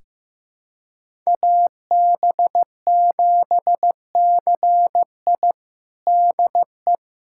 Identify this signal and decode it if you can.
Morse code, heard as a keyed tone, about 15 wpm, 700 Hz. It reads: AB7CI DE
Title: USA Callsign DE Callsign 15wpm